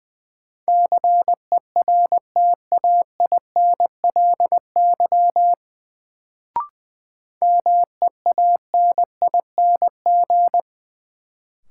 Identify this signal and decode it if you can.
Morse code, heard as a keyed tone, about 20 wpm, 700 Hz. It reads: CERTAINLY MEANING
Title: Sentences from Top 900 Words - New Word Drill - 20wpm